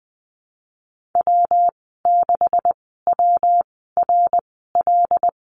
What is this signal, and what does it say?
Morse code, heard as a keyed tone, about 20 wpm, spaced as for 15 wpm, 700 Hz. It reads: W6WRL
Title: 186 Lesson 30g - Letter b - Call Signs 20wpm